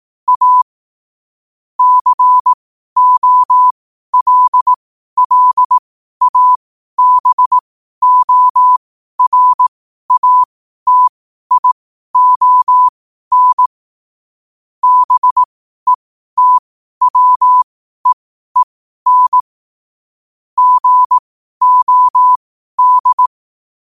Morse code, heard as a keyed tone, about 18 wpm, spaced as for 12 wpm, 1000 Hz. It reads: A COLLABORATION BETWEEN GOD
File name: qotdfiles / 12wpm